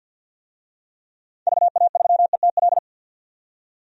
Morse code, heard as a keyed tone, about 52 wpm, 700 Hz. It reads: VA3ETL
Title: DXCC-100 Call Signs - 6 Characters - 50wpm